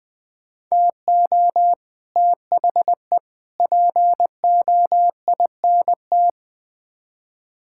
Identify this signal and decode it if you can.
Morse code, heard as a keyed tone, about 20 wpm, 700 Hz. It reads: TO THE POINT
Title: Build A Phrase - 4-Word Phrases - 20wpm